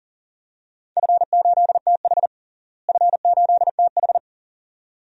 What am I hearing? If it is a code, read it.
Morse code, heard as a keyed tone, about 40 wpm, 700 Hz. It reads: F8TH F8TH
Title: DXCC-100 Call Signs - 4 Characters - Repeated 2x - 40wpm